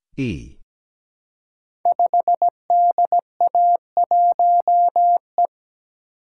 Morse code, heard as a keyed tone, about 17 wpm, 700 Hz. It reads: 5DA1E
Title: Code Groups - 5 Letter:Numbers - 17wpm